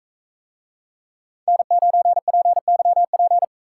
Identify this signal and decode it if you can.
Morse code, heard as a keyed tone, about 39 wpm, 700 Hz. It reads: N9PYP